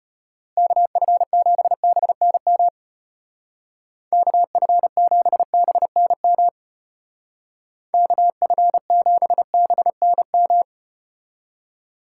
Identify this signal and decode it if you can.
Morse code, heard as a keyed tone, about 34 wpm, 700 Hz. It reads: KF7BNM KF7BNM KF7BNM